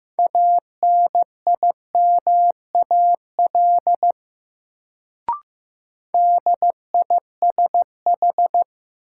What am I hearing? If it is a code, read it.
Morse code, heard as a keyed tone, about 15 wpm, 700 Hz. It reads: ANIMAL DISH
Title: Sentences from Top 1300 Words - New Word Drill Alt - 15wpm